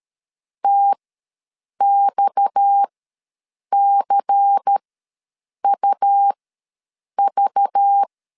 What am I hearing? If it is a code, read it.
Morse code, heard as a keyed tone, about 13 wpm, spaced as for 7 wpm, 800 Hz. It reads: TXCUV
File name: LEZIONI / 6 PKX0 / 6 PKX ws 12wpm 3106